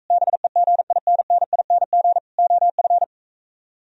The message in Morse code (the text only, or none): BEGINNING OF